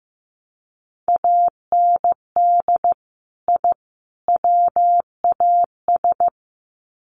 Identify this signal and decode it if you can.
Morse code, heard as a keyed tone, about 15 wpm, 700 Hz. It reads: AND I WAS
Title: Sets of 3 Words 15wpm